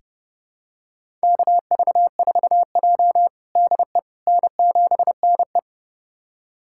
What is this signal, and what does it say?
Morse code, heard as a keyed tone, about 32 wpm, 700 Hz. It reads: KV4J DE N7NE
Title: USA Callsign DE Callsign - 4 Characters - 30wpm